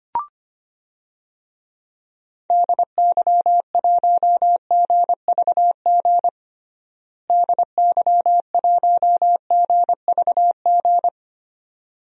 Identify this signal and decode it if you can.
Morse code, heard as a keyed tone, about 25 wpm, 700 Hz. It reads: DY1GVG DY1GVG
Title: DXCC-100 Call Signs - 6 Characters - Repeated 2x - 25wpm